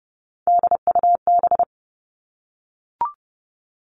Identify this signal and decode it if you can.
Morse code, heard as a keyed tone, about 30 wpm, 700 Hz. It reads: DUB
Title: Three Challenging Letter-Number Combinations 30wpm